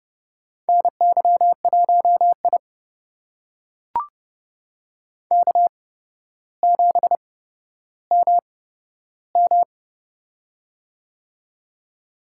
Morse code, heard as a keyed tone, about 31 wpm, spaced as for 13 wpm, 700 Hz. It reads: NY1I K7MM
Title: USA Call Signs - 4 Characters - 8x Character Spacing - 30wpm